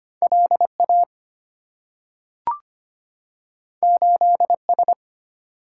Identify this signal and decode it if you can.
Morse code, heard as a keyed tone, about 25 wpm, 690 Hz. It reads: LA 8S
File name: Two Letter-Number Combinations - 25wpm